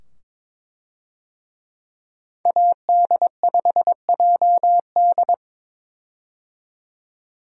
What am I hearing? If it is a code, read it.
Morse code, heard as a keyed tone, about 21 wpm, 700 Hz. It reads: AD5JD
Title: USA Call Signs - 5 Characters - 22wpm